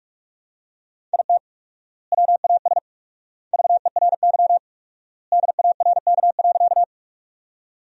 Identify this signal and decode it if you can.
Morse code, heard as a keyed tone, about 45 wpm, 700 Hz. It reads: IT WAS VERY DARK.
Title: 4x Word-Spacing - Sentences from Top 500 Words - 45wpm